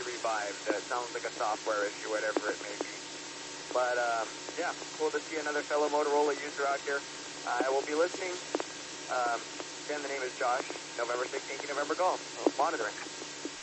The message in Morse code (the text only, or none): none